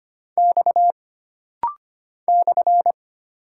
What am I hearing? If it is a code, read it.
Morse code, heard as a keyed tone, about 25 wpm, 700 Hz. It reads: X /
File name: Single Character (Letter:Number:Punctuation:Prosign) - Rapid-Fire - 25wpm